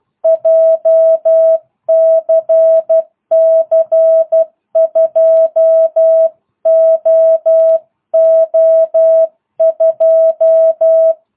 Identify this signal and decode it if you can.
Morse code, heard as a keyed tone, about 12 wpm, 650 Hz. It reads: JCC 2OO2